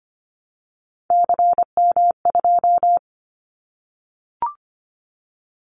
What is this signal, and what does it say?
Morse code, heard as a keyed tone, about 25 wpm, 700 Hz. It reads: CM2
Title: Callsign Prefixes for Top 100 DXCC - 25wpm